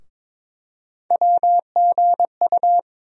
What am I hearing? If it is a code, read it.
Morse code, heard as a keyed tone, about 22 wpm, 700 Hz. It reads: WGU
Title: USA Call Sign Postfixes - 3 Letters 22wpm